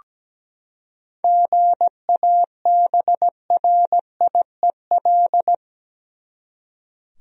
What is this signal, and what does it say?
Morse code, heard as a keyed tone, about 17 wpm, 700 Hz. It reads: GABRIEL
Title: USA Common Names - 17wpm